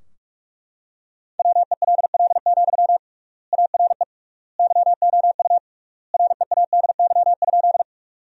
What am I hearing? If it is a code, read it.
Morse code, heard as a keyed tone, about 41 wpm, 700 Hz. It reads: WELL, ARE YOU READY?
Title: 3x Word-Spacing - Sentences from Top 400 Words - 45wpm